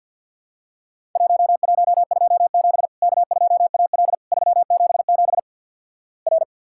Morse code, heard as a keyed tone, about 48 wpm, 700 Hz. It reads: 1127 K2AL 377 R
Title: Worked All Europe DX Contest - QTC Exchange - 50wpm